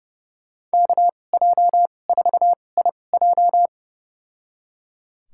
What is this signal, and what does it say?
Morse code, heard as a keyed tone, about 30 wpm, spaced as for 22 wpm, 700 Hz. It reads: KJ4IJ
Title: USA Call Signs - 5 Characters - 2x Character Spacing - 30wpm